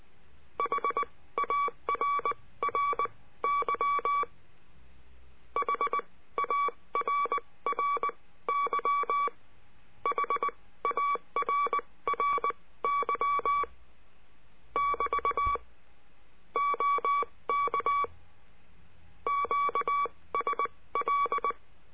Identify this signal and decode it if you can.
Morse code, heard as a keyed tone, about 19 wpm, spaced as for 15 wpm, 1140 Hz. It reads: HARRY HARRY HARRY <BT> OK QSL
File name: CURSO DE TELEGRAFIA